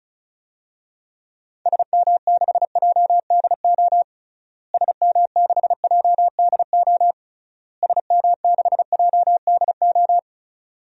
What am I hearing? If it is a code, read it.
Morse code, heard as a keyed tone, about 36 wpm, 700 Hz. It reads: SM6JDO SM6JDO SM6JDO
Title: DXCC-100 Call Signs - 6 Characters - Repeated 3x - 35wpm